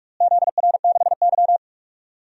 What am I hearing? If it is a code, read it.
Morse code, heard as a keyed tone, about 46 wpm, 700 Hz. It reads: ZR6Y
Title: DXCC-100 Call Signs - 4 Characters - Repeated 2x - 45wpm